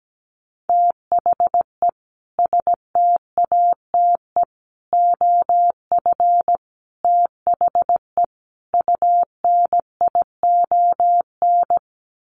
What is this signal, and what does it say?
Morse code, heard as a keyed tone, about 17 wpm, 700 Hz. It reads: THE STATE OF THE UNION
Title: Sets of 5 Words Encore 17wpm